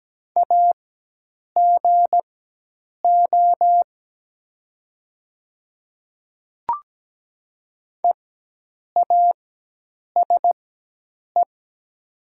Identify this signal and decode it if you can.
Morse code, heard as a keyed tone, about 17 wpm, spaced as for 8 wpm, 700 Hz. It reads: AGO EASE
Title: 4x Character Spacing - Top 500 Words - 17wpm